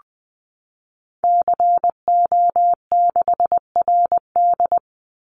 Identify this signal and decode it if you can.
Morse code, heard as a keyed tone, about 19 wpm, 700 Hz. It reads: CO6RD
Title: Call Signs Repeated 2x - 20wpm